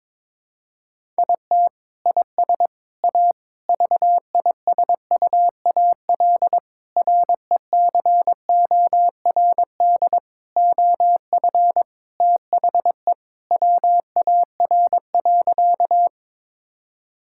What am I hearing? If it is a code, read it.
Morse code, heard as a keyed tone, about 22 wpm, 700 Hz. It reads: IT IS A VISUAL RECORD OF THE WAR.
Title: Sentences from Top 1000 Words - Review Part 6 - 22wpm